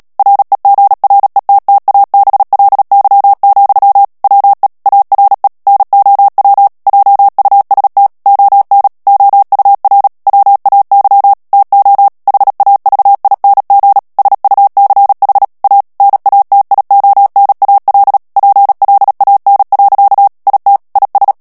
Morse code, heard as a keyed tone, about 37 wpm, 800 Hz. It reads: REGRETTABLY, WE ARE NOW JUST ON OUR WAY TO HAVING SUCH A NATIONAL PLAN. IT IS
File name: morse mp3 / ANGLAIS texte clair 2 vitesse 35 mots minute